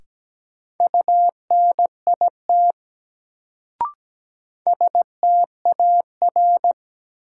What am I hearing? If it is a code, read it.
Morse code, heard as a keyed tone, about 17 wpm, 700 Hz. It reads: UNIT STAR
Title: Sentences from Top 1000 Words - Review Part 2 - New Word Drill - 17wpm